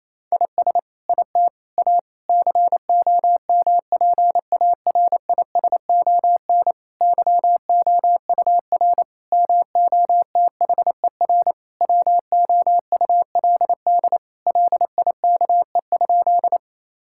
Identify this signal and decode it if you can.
Morse code, heard as a keyed tone, about 27 wpm, 700 Hz. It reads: IS IT A COMPARISON YOUR MOTHER WOULD LIKE?
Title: Sentences from Top 1600 Words 28wpm